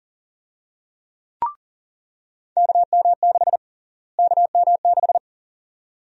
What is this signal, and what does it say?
Morse code, heard as a keyed tone, about 40 wpm, 700 Hz. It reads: KM6 KM6
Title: USA Call Sign Prefixes - Repeated 2x - 40wpm